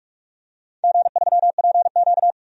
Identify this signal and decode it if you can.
Morse code, heard as a keyed tone, about 46 wpm, 700 Hz. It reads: G3PQ
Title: DXCC-100 Call Signs - 4 Characters - 45wpm